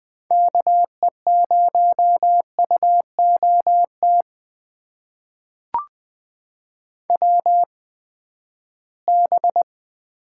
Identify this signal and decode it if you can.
Morse code, heard as a keyed tone, about 20 wpm, spaced as for 14 wpm, 700 Hz. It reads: KE0UOT WB
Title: USA Call Signs - 6 Characters - 8x Character Spacing - 20wpm